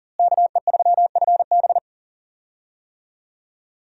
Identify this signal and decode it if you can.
Morse code, heard as a keyed tone, about 40 wpm, 700 Hz. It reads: KE3FB
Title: USA Call Signs - 5 Characters - Repeated 2x - 40wpm